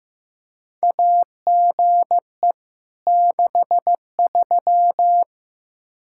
Morse code, heard as a keyed tone, about 15 wpm, 700 Hz. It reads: AGE 63
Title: Practice QSO - Exchange 3 - More Info - 15wpm